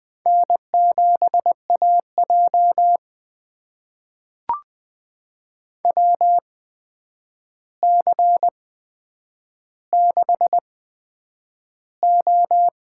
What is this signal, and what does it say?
Morse code, heard as a keyed tone, about 20 wpm, spaced as for 9 wpm, 700 Hz. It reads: N7AJ WC6O
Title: USA Call Signs - 4 Characters - 8x Character Spacing - 20wpm